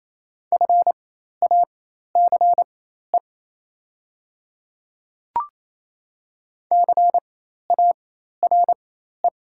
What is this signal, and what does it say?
Morse code, heard as a keyed tone, about 28 wpm, spaced as for 13 wpm, 700 Hz. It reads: FACE CARE
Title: 4x Character Spacing - Top 300 Words - 28wpm